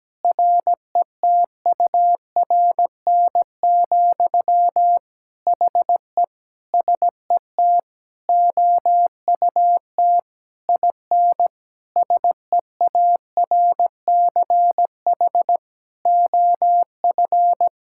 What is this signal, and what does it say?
Morse code, heard as a keyed tone, about 17 wpm, 700 Hz. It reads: RETURN, HE SET OUT IN SEARCH OF
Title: Sentences from Top 1000 Words - Review Part 5 - 17wpm